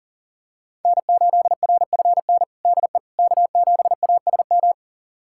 Morse code, heard as a keyed tone, about 37 wpm, 700 Hz. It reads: N8RFN DE K7ASM
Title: USA Callsign DE Callsign - 5 Characters - 40wpm